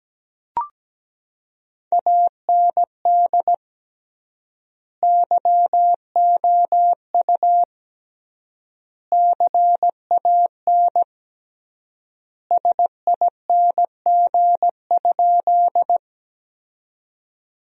Morse code, heard as a keyed tone, about 17 wpm, 700 Hz. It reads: AND YOU CAN SING?
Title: 3x Word-Spacing - Sentences from Top 400 Words - 17wpm